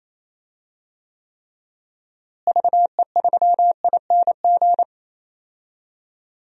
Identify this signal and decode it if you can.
Morse code, heard as a keyed tone, about 28 wpm, 700 Hz. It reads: VE3ING
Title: DXCC-100 Call Signs - 6 Characters - Repeated 3x - 28wpm